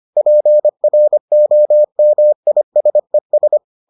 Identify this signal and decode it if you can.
Morse code, heard as a keyed tone, about 25 wpm, 600 Hz. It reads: PROMISES